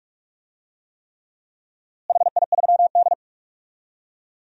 Morse code, heard as a keyed tone, about 42 wpm, 700 Hz. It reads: HI3D
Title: DXCC-100 Call Signs - 4 Characters - Repeated 3x - 45wpm